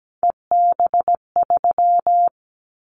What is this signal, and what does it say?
Morse code, heard as a keyed tone, about 17 wpm, 700 Hz. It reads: EB3
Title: Callsign Prefixes for Top 25 DXCC - Repeated 3x - 17wpm